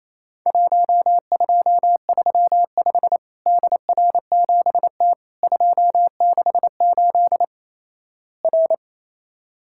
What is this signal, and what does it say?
Morse code, heard as a keyed tone, about 28 wpm, 700 Hz. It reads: 1235 DR7T 268 R
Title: Worked All Europe DX Contest - QTC Exchange - 28wpm